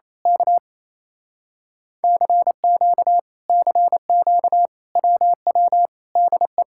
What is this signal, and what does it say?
Morse code, heard as a keyed tone, about 27 wpm, 700 Hz. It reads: K CQ CQ WW DE
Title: Speed-Racing Common QSO Phrases - 28wpm